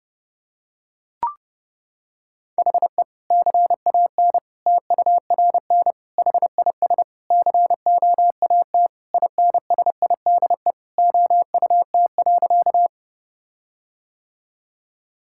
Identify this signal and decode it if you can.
Morse code, heard as a keyed tone, about 30 wpm, 700 Hz. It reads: HE CAN TURN HIS COAT INSIDE OUT.
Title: Sentences from Top 1600 Words 30wpm